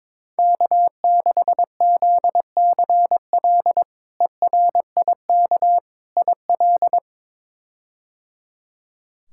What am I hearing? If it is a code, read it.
Morse code, heard as a keyed tone, about 22 wpm, 700 Hz. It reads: K6ZCL ERIK IL
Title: K1USN SST QSO Practice - 22wpm